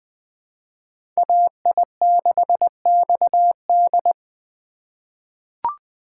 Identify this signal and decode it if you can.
Morse code, heard as a keyed tone, about 21 wpm, 700 Hz. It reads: AI6XD
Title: USA Call Signs - 5 Characters - 3x Character Spacing - 20wpm